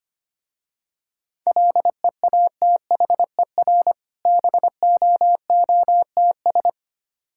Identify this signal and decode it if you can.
Morse code, heard as a keyed tone, about 25 wpm, 700 Hz. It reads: LEATHER BOOTS
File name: practice / 25wpm / Difficult Words with Hints - 25wpm